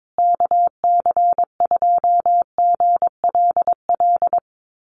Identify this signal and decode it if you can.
Morse code, heard as a keyed tone, about 22 wpm, 700 Hz. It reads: KC2GLL
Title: USA Call Signs Repeated 3x 22wpm